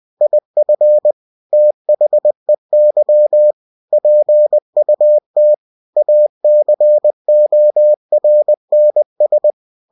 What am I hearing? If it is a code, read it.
Morse code, heard as a keyed tone, about 20 wpm, 600 Hz. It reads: IF THEY PUT ACORNS
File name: cw-books / winnie-the-pooh / winnie-20wpm-page-0037